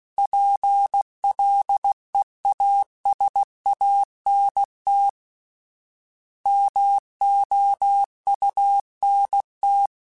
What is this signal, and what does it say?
Morse code, heard as a keyed tone, about 16 wpm, 800 Hz. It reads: PLEASANT MOUNT
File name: fromSMs / FromSMs mp3s / Cities@15